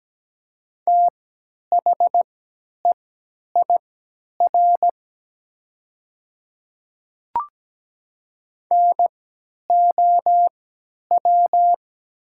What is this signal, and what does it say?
Morse code, heard as a keyed tone, about 17 wpm, spaced as for 10 wpm, 700 Hz. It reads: THEIR NOW
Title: 3x Character Spacing - Top 100 Words - 17wpm